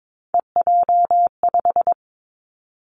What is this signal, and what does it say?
Morse code, heard as a keyed tone, about 22 wpm, 700 Hz. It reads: EJ5
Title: Three Challenging Letter-Number Combinations 22wpm